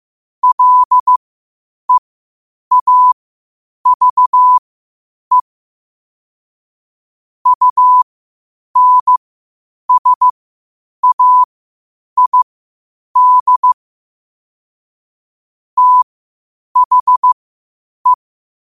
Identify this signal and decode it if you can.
Morse code, heard as a keyed tone, about 15 wpm, spaced as for 8 wpm, 1000 Hz. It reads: LEAVE UNSAID THE
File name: qotdfiles / 8wpm